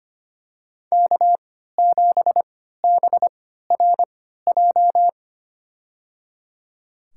Morse code, heard as a keyed tone, about 25 wpm, spaced as for 14 wpm, 700 Hz. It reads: K7BRJ
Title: USA Call Signs - 5 Characters - 3x Character Spacing - 25wpm